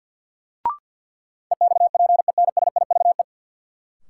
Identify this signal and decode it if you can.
Morse code, heard as a keyed tone, about 49 wpm, 700 Hz. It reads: EXPENSIVE
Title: Sentences from Top 1500 Words - New Word Drill Alt - 50wpm